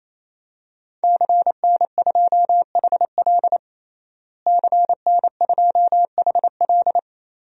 Morse code, heard as a keyed tone, about 27 wpm, 700 Hz. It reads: CN2HL CN2HL
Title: DXCC-100 Call Signs - 5 Characters - Repeated 3x - 28wpm